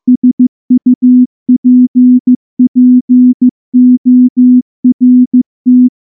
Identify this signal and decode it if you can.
Morse code, heard as a keyed tone, about 14 wpm, 250 Hz. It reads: SUPPORT
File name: morse / morse-training-15WPM